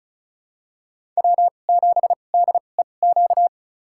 Morse code, heard as a keyed tone, about 36 wpm, spaced as for 25 wpm, 700 Hz. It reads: W7DEQ
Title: USA Call Signs - 5 Characters - 2x Character Spacing - 35wpm